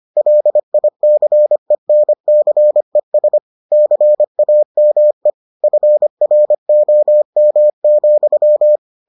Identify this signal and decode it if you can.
Morse code, heard as a keyed tone, about 25 wpm, 600 Hz. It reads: LICENCES CAME FROM,